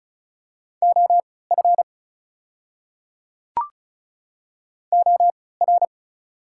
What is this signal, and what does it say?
Morse code, heard as a keyed tone, about 34 wpm, spaced as for 20 wpm, 700 Hz. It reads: OF OR
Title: 3x Character Spacing - Top 500 Words - 35wpm